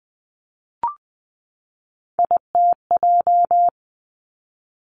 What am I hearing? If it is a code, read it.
Morse code, heard as a keyed tone, about 20 wpm, 700 Hz. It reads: ITJ
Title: Three Challenging Letter-Number Combinations 20wpm